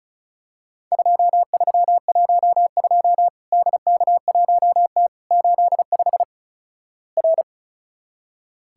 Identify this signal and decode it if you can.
Morse code, heard as a keyed tone, about 36 wpm, 700 Hz. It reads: 2312 DK1T 85 R
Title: Worked All Europe DX Contest - QTC Exchange - 35wpm